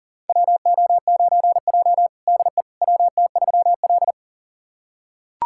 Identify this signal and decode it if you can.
Morse code, heard as a keyed tone, about 40 wpm, 700 Hz. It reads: WO9J DE WT3L
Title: USA Callsign DE Callsign - 4 Characters - 40wpm